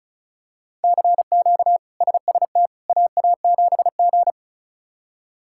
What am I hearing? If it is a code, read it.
Morse code, heard as a keyed tone, about 34 wpm, 700 Hz. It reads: CQ SST AA7G
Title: K1USN SST QSO Practice - 35wpm